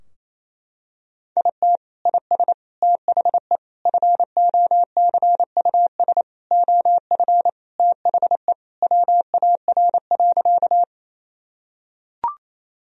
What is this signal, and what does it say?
Morse code, heard as a keyed tone, about 28 wpm, 700 Hz. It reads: IT IS THE FOCUS OF THE WAR.
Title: Sentences from Top 1000 Words - Review Part 4 - 28wpm